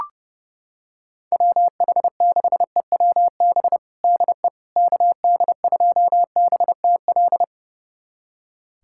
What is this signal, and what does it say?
Morse code, heard as a keyed tone, about 32 wpm, 690 Hz. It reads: WH6EWB DE KD2BTL
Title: USA Callsign DE Callsign - 6 Characters - 30wpm